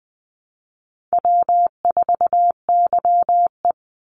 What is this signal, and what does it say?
Morse code, heard as a keyed tone, about 20 wpm, 700 Hz. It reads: W4YE
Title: Easy Call Signs Repeated 3x - 20wpm